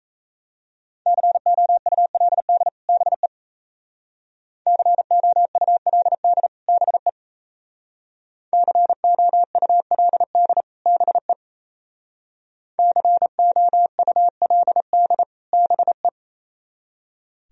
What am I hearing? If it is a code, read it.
Morse code, heard as a keyed tone, about 35 wpm, 690 Hz. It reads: COULD BE COULD BE COULD BE COULD BE
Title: Speed-Racing - Sets of 2 Words 28wpm